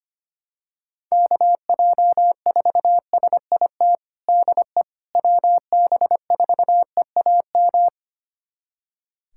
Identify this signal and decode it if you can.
Morse code, heard as a keyed tone, about 25 wpm, 700 Hz. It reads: KJ4SIT DE WB4EAM
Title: USA Callsign DE Callsign - 6 Characters - 25wpm